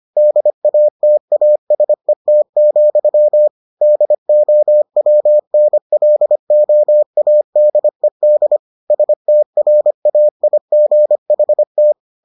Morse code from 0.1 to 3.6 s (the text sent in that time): DATASET,